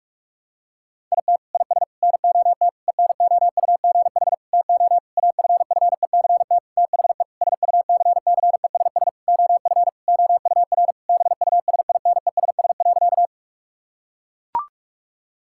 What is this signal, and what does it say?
Morse code, heard as a keyed tone, about 41 wpm, 700 Hz. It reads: IT IS NOT ENOUGH TO AFFECT THE SUCCESS OF OUR BUSINESS.
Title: Sentences from Top 1000 Words - Review Part 6 - 45wpm